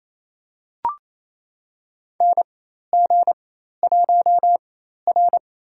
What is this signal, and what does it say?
Morse code, heard as a keyed tone, about 28 wpm, spaced as for 13 wpm, 700 Hz. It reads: NG1R
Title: USA Call Signs - 4x Character Spacing - 28wpm